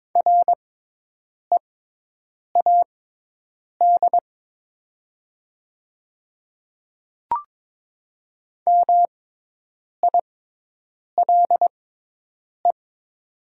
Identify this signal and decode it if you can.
Morse code, heard as a keyed tone, about 22 wpm, spaced as for 8 wpm, 700 Hz. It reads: READ MILE
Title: 6x Character Spacing - Top 300 Words - 22wpm